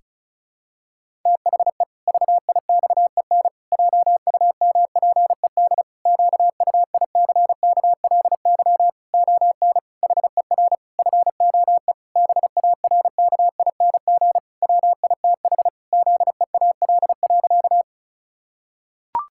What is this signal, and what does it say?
Morse code, heard as a keyed tone, about 36 wpm, 700 Hz. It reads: THE VIXEN JUMPED QUICKLY ON HER FOE BARKING WITH ZEAL.
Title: Pangram Challenge v2 - 35wpm